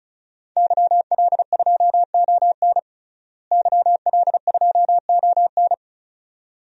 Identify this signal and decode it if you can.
Morse code, heard as a keyed tone, about 34 wpm, 700 Hz. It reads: YL2ON YL2ON
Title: DXCC-100 Call Signs - 5 Characters - Repeated 2x - 35wpm